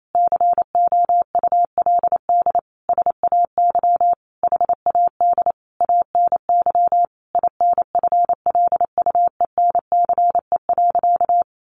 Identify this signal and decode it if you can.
Morse code, heard as a keyed tone, about 28 wpm, 700 Hz. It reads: COULD SAY HAD ANY INFLUENCE.
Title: Sentences from Top 1100 Words 28wpm